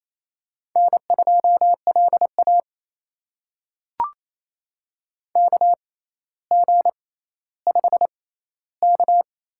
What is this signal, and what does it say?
Morse code, heard as a keyed tone, about 28 wpm, spaced as for 14 wpm, 700 Hz. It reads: N2LA KG5K
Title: USA Call Signs - 6x Character Spacing - 28wpm